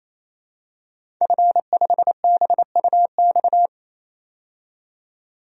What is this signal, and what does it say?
Morse code, heard as a keyed tone, about 28 wpm, 700 Hz. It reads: F5BUX
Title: DXCC-100 Call Signs - 5 Characters - 28wpm